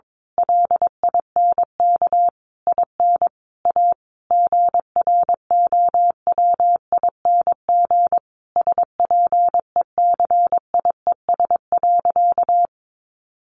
Morse code, heard as a keyed tone, about 22 wpm, 700 Hz. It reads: LINK IN A GROWING SPECIES.
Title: A Sound Of Thunder 22wpm